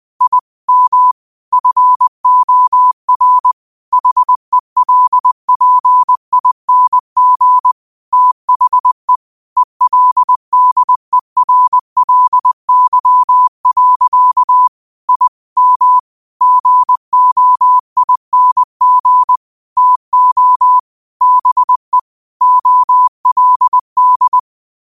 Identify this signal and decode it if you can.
Morse code, heard as a keyed tone, about 20 wpm, 1000 Hz. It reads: I M FOR HELPING THE ELDERLY. I M GOING TO BE OLD